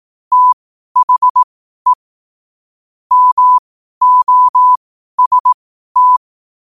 Morse code, heard as a keyed tone, about 18 wpm, spaced as for 12 wpm, 1000 Hz. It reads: THE MOST
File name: 12 wpm